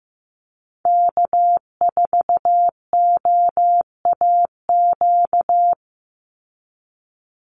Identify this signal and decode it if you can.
Morse code, heard as a keyed tone, about 15 wpm, 700 Hz. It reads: K4OAQ
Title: Call Signs Repeated 3x - 15wpm